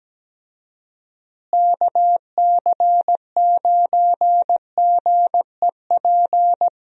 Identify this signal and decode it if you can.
Morse code, heard as a keyed tone, about 17 wpm, 700 Hz. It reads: KC9GEP
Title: USA Call Signs - 6 Characters - Repeated 2x - 17wpm